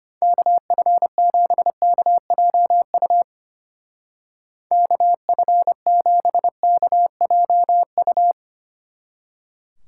Morse code, heard as a keyed tone, about 31 wpm, 700 Hz. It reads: KF7KJU KF7KJU
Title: Speed Racing - USA Call Signs 25wpm